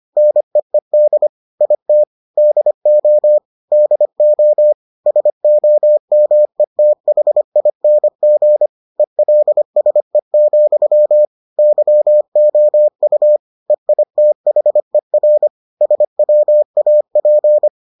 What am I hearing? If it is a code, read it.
Morse code, heard as a keyed tone, about 25 wpm, 600 Hz. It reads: NEED IT DO DO SOMETHING ELSE, YOU EITHER SWAP